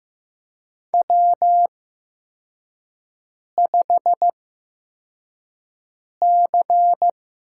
Morse code, heard as a keyed tone, about 15 wpm, 700 Hz. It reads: W 5 C
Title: USA Call Signs - 8x Character Spacing - 15wpm